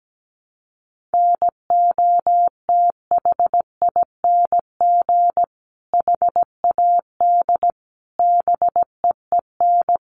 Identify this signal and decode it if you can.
Morse code, heard as a keyed tone, about 17 wpm, 700 Hz. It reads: NOTHING HAD BEEN
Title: Sentences from Top 1300 Words 17wpm